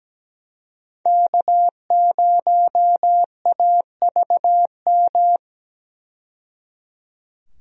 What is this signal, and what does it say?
Morse code, heard as a keyed tone, about 17 wpm, 700 Hz. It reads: K0AVM
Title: USA Call Signs - 5 Characters - 17wpm